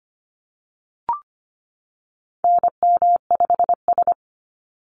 Morse code, heard as a keyed tone, about 25 wpm, 700 Hz. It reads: NM5S
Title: NA SOTA Activator Callsigns 25wpm